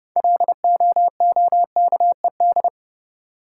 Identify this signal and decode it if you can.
Morse code, heard as a keyed tone, about 30 wpm, 700 Hz. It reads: LOOKED
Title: Sentences from Top 800 Words - New Word Drill - 30wpm